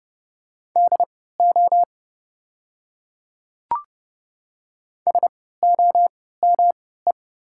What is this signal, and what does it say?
Morse code, heard as a keyed tone, about 30 wpm, spaced as for 17 wpm, 700 Hz. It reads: DO SOME
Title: 3x Character Spacing - Top 200 Words - 30wpm